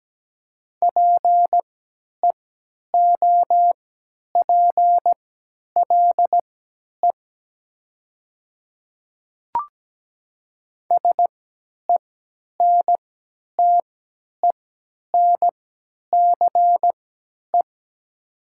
Morse code, heard as a keyed tone, about 17 wpm, spaced as for 10 wpm, 700 Hz. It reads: PEOPLE SENTENCE
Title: 3x Character Spacing - Top 500 Words - 17wpm